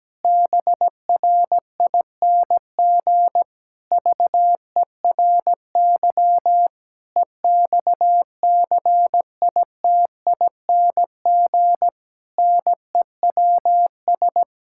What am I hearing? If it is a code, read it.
Morse code, heard as a keyed tone, about 17 wpm, 700 Hz. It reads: BRING VERY EXCITING NEWS
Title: Pangram Challenge v2 - 17wpm